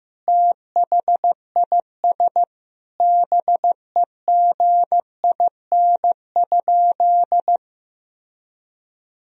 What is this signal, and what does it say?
Morse code, heard as a keyed tone, about 16 wpm, 700 Hz. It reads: THIS BEGIN?
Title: Sentences from Top 1000 Words - Review Part 1 - 15wpm